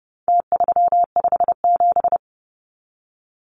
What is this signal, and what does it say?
Morse code, heard as a keyed tone, about 30 wpm, 700 Hz. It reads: T357
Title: Numbers - Contest Style - 4-Digit 30wpm